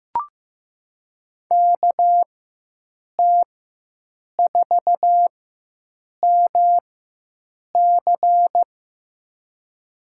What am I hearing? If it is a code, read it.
Morse code, heard as a keyed tone, about 15 wpm, spaced as for 7 wpm, 700 Hz. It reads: KT4MC
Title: USA Call Signs - 5 Characters - 4x Character Spacing - 15wpm